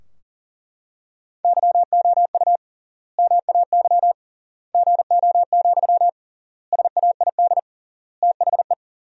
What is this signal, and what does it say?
Morse code, heard as a keyed tone, about 40 wpm, 700 Hz. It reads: YOU MAY GO, SAID THE
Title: 3x Word-Spacing - Sentences from Top 400 Words - 40wpm